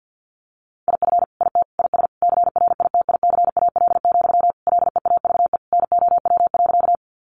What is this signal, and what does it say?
Morse code, heard as a keyed tone, about 50 wpm, 700 Hz. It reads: IF IT IS CRITICAL, LEAVE NOW.
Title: Sentences from Top 1100 Words 50wpm